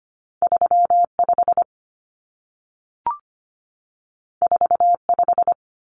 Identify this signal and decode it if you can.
Morse code, heard as a keyed tone, about 25 wpm, 700 Hz. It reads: 35 45
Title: Numbers - 2-Digit 25wpm